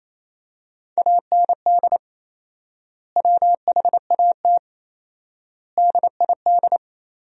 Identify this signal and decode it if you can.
Morse code, heard as a keyed tone, about 28 wpm, 700 Hz. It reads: AND WHAT DID
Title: 4x Word-Spacing - Sentences from Top 100 Words - 28wpm